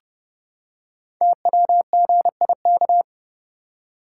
Morse code, heard as a keyed tone, about 30 wpm, 700 Hz. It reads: TWGIK
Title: Code Groups - 5 Letters - 30wpm